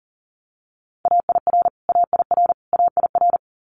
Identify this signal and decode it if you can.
Morse code, heard as a keyed tone, about 38 wpm, 700 Hz. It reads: AIR AIR AIR
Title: Multi-letter Phonemes Repeated 3x 40wpm